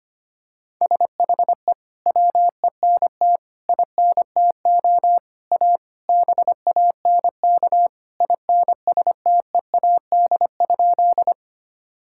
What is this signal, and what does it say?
Morse code, heard as a keyed tone, about 25 wpm, 700 Hz. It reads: SHE WENT INTO A BANK INSTEAD?
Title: Sentences from Top 1000 Words - Review Part 6 - 25wpm